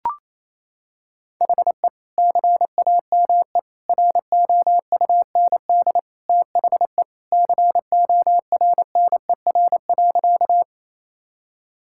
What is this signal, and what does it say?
Morse code, heard as a keyed tone, about 28 wpm, 700 Hz. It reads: HE CAME ROUND THE CORNER.
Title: Sentences from Top 1600 Words 28wpm